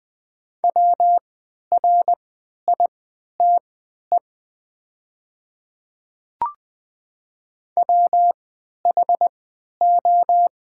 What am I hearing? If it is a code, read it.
Morse code, heard as a keyed tone, about 20 wpm, spaced as for 11 wpm, 700 Hz. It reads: WRITE WHO
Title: 3x Character Spacing - Top 100 Words - 20wpm